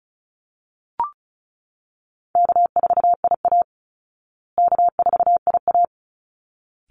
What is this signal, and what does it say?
Morse code, heard as a keyed tone, about 34 wpm, 700 Hz. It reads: K4IA K4IA
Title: Easy Call Signs Repeated 2x - 35wpm